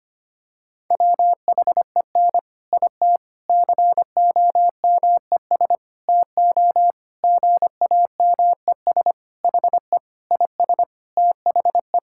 Morse code, heard as a keyed tone, about 25 wpm, 700 Hz. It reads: WHEN IT COMES TO GAMES HE IS THE
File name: practice / 25wpm / Sentences with 3-Word Phrases - 25wpm